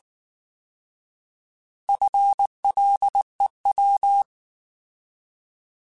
Morse code, heard as a keyed tone, about 19 wpm, 800 Hz. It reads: FLEW